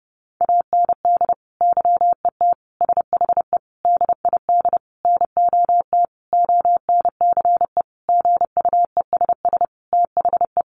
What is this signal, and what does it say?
Morse code, heard as a keyed tone, about 30 wpm, 700 Hz. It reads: AND YET SHE DID NOT ONCE GUESS THE
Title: Sentences from Top 1500 Words 30wpm